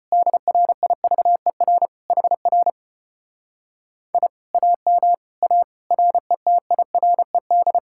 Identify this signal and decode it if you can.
Morse code, heard as a keyed tone, about 33 wpm, 700 Hz. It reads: DRIVER HR I AM A RETIRED